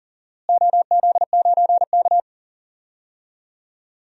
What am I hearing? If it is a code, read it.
Morse code, heard as a keyed tone, about 40 wpm, 700 Hz. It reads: OZ9K
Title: DXCC-100 Call Signs - 4 Characters - Repeated 3x - 40wpm